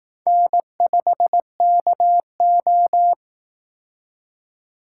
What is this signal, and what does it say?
Morse code, heard as a keyed tone, about 18 wpm, 700 Hz. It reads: N5KO
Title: Speed Racing - USA 1x2 Call Signs 15wpm